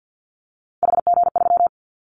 Morse code, heard as a keyed tone, about 50 wpm, 700 Hz. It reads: HG3
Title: Callsign Prefixes for Top 25 DXCC - 50wpm